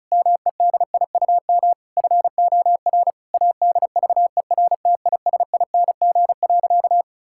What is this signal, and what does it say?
Morse code, heard as a keyed tone, about 34 wpm, 690 Hz. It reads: MEDIUM FOR ADVERTISING.
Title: Sentences from Top 1000 Words - Review Part 6 - 35wpm